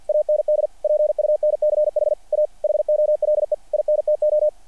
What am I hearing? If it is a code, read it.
Morse code, heard as a keyed tone, about 48 wpm, 590 Hz. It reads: AND PUNCH A HOLE INTO
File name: archived / CRSnet